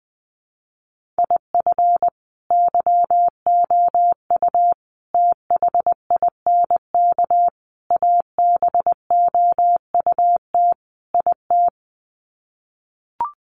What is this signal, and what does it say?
Morse code, heard as a keyed tone, about 19 wpm, 700 Hz. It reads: IF YOU THINK ABOUT IT
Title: Sets of 5 Words Encore2 20wpm